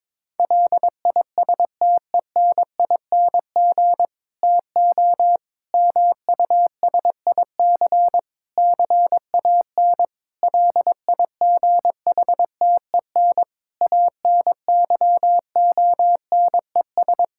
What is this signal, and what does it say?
Morse code, heard as a keyed tone, about 22 wpm, 700 Hz. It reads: LISTENING TO MUSIC CAN LIGHTEN ANYONES
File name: Difficult Sentences with Hints - 22wpm